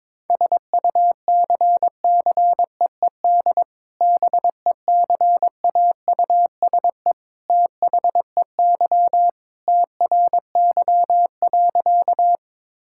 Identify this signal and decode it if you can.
Morse code, heard as a keyed tone, about 22 wpm, 700 Hz. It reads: SUCCEED BECAUSE THEY TRY.
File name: practice / 22wpm / Sentences with 2-Word Phrases - 22wpm